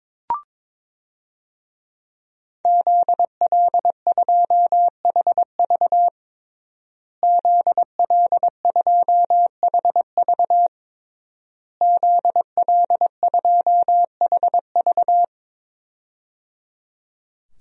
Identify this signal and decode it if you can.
Morse code, heard as a keyed tone, about 22 wpm, 700 Hz. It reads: ZL2HV ZL2HV ZL2HV
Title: DXCC-100 Call Signs - 5 Characters - Repeated 3x - 22wpm